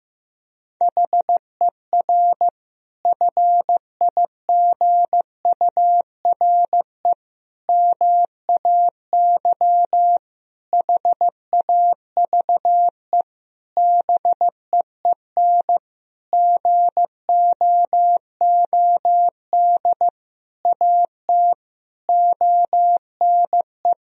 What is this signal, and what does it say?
Morse code, heard as a keyed tone, about 15 wpm, 700 Hz. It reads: HER FIGURE MAY HAVE BEEN GOOD AT ONE
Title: Sentences from Top 1000 Words - Review Part 2 - 15wpm